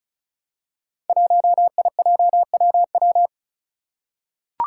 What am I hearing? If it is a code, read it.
Morse code, heard as a keyed tone, about 36 wpm, 700 Hz. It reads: 1IJWW
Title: Five Challenging Letter-Number Combinations 35wpm